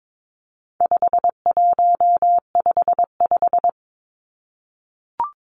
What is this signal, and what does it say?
Morse code, heard as a keyed tone, about 22 wpm, 700 Hz. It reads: 5155
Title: Numbers - 4-Digit 22wpm